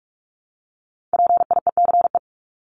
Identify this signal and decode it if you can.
Morse code, heard as a keyed tone, about 44 wpm, 700 Hz. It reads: PIECE
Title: Top 500 Words 45wpm